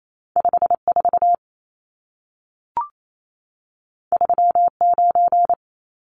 Morse code, heard as a keyed tone, about 28 wpm, 700 Hz. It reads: 54 39
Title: Numbers - 2-Digit 28wpm